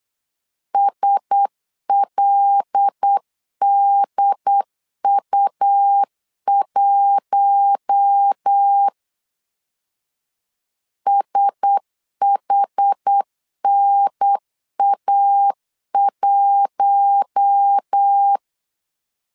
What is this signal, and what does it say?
Morse code, heard as a keyed tone, about 8 wpm, 800 Hz. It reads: SLDU1 SHNA1
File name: LEZIONI / 5 LBVC / 5 L 8wpm 2831